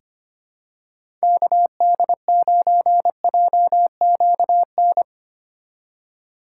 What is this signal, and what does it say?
Morse code, heard as a keyed tone, about 25 wpm, 700 Hz. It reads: KD9JQN
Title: USA Call Signs - 4x Character Spacing - 25wpm